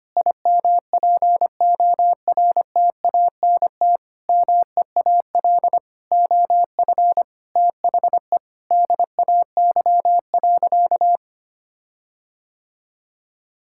Morse code, heard as a keyed tone, about 25 wpm, 700 Hz. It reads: IMPORTANT MEAL OF THE DAY.
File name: practice / 25wpm / Sentences with 9-Word Phrases - 25wpm